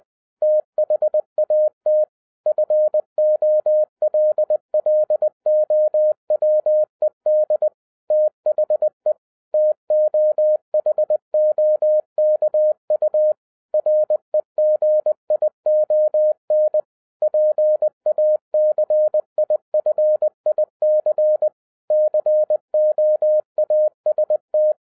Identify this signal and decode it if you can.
Morse code, heard as a keyed tone, about 20 wpm, 610 Hz. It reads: THAT FOLLOWED THE TOHOKU REGION PACIFIC COAST
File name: 2011-10-mp3 / n03-TARO JR0QWW 20wpm